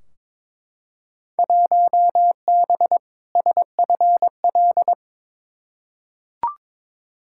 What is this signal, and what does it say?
Morse code, heard as a keyed tone, about 22 wpm, 700 Hz. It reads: 1B SFL
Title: Field Day Exchanges Encore - 22wpm